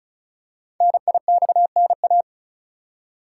Morse code, heard as a keyed tone, about 36 wpm, 700 Hz. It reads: NIXNA
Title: Code Groups - 5 Letters - 35wpm